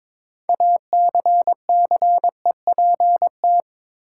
Morse code, heard as a keyed tone, about 22 wpm, 700 Hz. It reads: ACCEPT
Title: Sentences from Top 1300 Words - New Word Drill - 22wpm